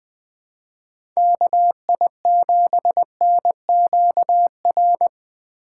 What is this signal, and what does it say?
Morse code, heard as a keyed tone, about 20 wpm, 700 Hz. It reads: KI7NQR
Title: USA Call Signs - 6 Characters - 4x Character Spacing - 20wpm